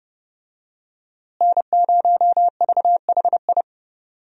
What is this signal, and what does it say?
Morse code, heard as a keyed tone, about 32 wpm, 700 Hz. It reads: N0VHI
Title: USA Call Signs - 5 Characters - Repeated 2x - 30wpm